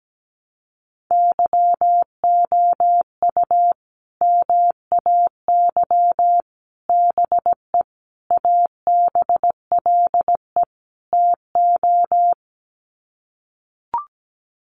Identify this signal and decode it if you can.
Morse code, heard as a keyed tone, about 17 wpm, 700 Hz. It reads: YOU MAY BE ABLE TO
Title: Sets of 5 Words Encore 17wpm